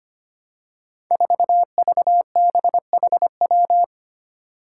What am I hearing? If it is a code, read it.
Morse code, heard as a keyed tone, about 25 wpm, 700 Hz. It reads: 4VBHW
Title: Five Challenging Letter-Number Combinations 25wpm